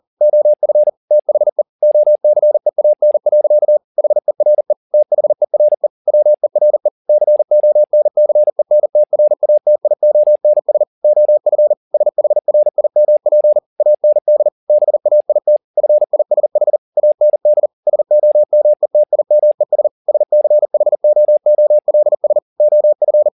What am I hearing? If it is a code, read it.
Morse code, heard as a keyed tone, about 40 wpm, 600 Hz. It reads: OF THE OCEAN. HERE THERE WERE CONCENTRATIONS OF SHRIMP AND BAIT FISH AND SOMETIMES SCHOOLS OF